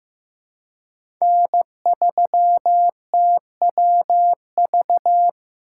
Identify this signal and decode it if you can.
Morse code, heard as a keyed tone, about 15 wpm, 700 Hz. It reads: N3TWV
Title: USA Call Signs - 2x Character Spacing - 15wpm